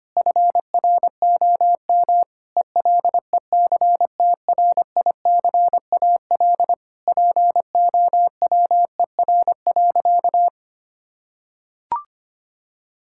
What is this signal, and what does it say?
Morse code, heard as a keyed tone, about 25 wpm, 700 Hz. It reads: FROM ELECTRICAL POWER.
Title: Sentences from Top 1000 Words - Review Part 6 - 25wpm